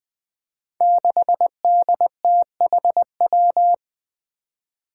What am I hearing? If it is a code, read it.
Morse code, heard as a keyed tone, about 20 wpm, 700 Hz. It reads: 6DTHW
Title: Five Challenging Letter-Number Combinations 20wpm